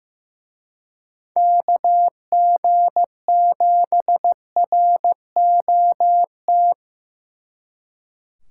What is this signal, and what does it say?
Morse code, heard as a keyed tone, about 18 wpm, 700 Hz. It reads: KG7ROT
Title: Speed Racing - USA 2x3 Call Signs 15wpm